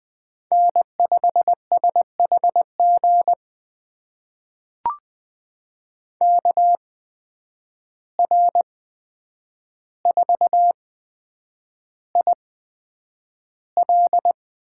Morse code, heard as a keyed tone, about 20 wpm, spaced as for 9 wpm, 700 Hz. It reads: N5SHG KR4IL
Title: USA Call Signs - 5 Characters - 8x Character Spacing - 20wpm